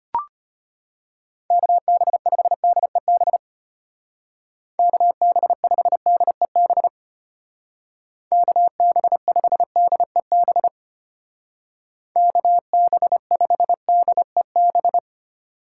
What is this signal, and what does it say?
Morse code, heard as a keyed tone, about 30 wpm, 700 Hz. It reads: KB5DEB KB5DEB KB5DEB KB5DEB
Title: Speed Racing - USA Call Signs 25wpm